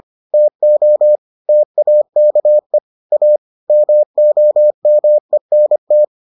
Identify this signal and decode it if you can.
Morse code, heard as a keyed tone, about 25 wpm, 600 Hz. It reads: TO TAKE A MOMENT